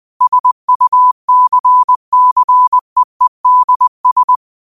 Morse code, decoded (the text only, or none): SUCCEEDS